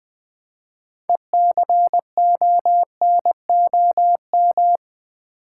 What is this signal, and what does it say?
Morse code, heard as a keyed tone, about 20 wpm, 700 Hz. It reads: ECONOM
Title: Build A Word - 3 Letters At A Time - 9-Letter Words - 20wpm